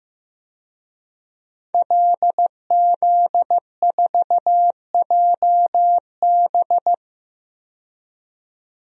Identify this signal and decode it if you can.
Morse code, heard as a keyed tone, about 15 wpm, 700 Hz. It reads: LZ4JB
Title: DXCC-100 Call Signs - 5 Characters - 15wpm